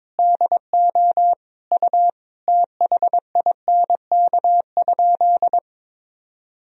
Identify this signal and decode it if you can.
Morse code, heard as a keyed tone, about 22 wpm, 700 Hz. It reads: DO U THINK?
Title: Sentences from Top 200 Words and Abbreviations 22wpm